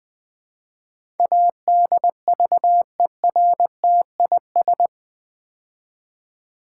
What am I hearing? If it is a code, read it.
Morse code, heard as a keyed tone, about 20 wpm, 700 Hz. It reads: ADVERTIS
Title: Build A Word - 4 Letters At A Time - 11-Letter Words - 20wpm